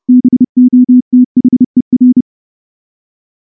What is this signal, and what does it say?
Morse code, heard as a keyed tone, about 30 wpm, 250 Hz. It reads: BOTHER
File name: morse / morse-training-30WPM